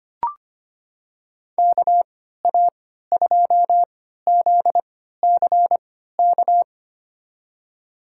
Morse code, heard as a keyed tone, about 25 wpm, spaced as for 14 wpm, 700 Hz. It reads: KA2ZCK
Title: USA Call Signs - 3x Character Spacing - 25wpm